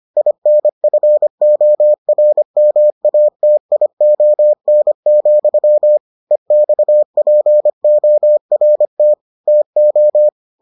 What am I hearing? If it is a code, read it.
Morse code, heard as a keyed tone, about 25 wpm, 600 Hz. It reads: INFORMATION, EXPORT TO